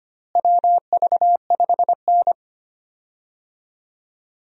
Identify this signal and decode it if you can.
Morse code, heard as a keyed tone, about 25 wpm, 700 Hz. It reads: WV5N
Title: USA Call Signs - 4 Characters - Repeated 3x - 25wpm